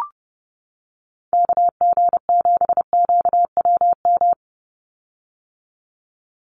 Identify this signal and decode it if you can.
Morse code, heard as a keyed tone, about 30 wpm, 700 Hz. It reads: KG7QWM
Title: USA Call Signs 30wpm